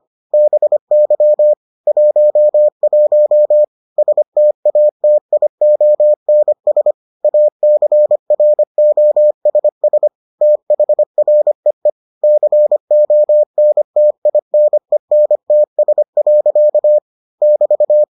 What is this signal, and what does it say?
Morse code, heard as a keyed tone, about 25 wpm, 600 Hz. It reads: BY 11 STATIONS ACROSS THREE CONTINENTS. <BT>